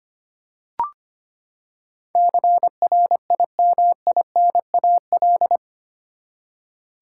Morse code, heard as a keyed tone, about 25 wpm, 700 Hz. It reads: CRIMINAL
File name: Eight Letter Words - 25wpm